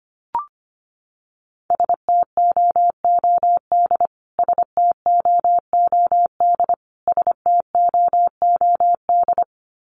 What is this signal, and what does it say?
Morse code, heard as a keyed tone, about 25 wpm, 700 Hz. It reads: STOOD STOOD STOOD
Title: Top 500 Words Repeated 25wpm